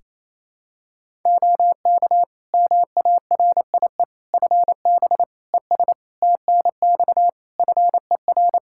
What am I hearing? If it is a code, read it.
Morse code, heard as a keyed tone, about 28 wpm, 700 Hz. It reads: OK MARIE FB ES TNX FER
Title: Practice QSO - Exchange 2 - Info - 28wpm